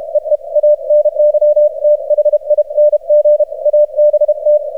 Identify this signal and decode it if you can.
Morse code, heard as a keyed tone, about 32 wpm, 600 Hz. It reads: EE ANYTHING ABT
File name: archived / CRSnet